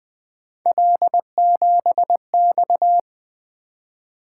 Morse code, heard as a keyed tone, about 25 wpm, 700 Hz. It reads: L7X
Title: Speed-Racing - Three Letter-Number Combinations - 20wpm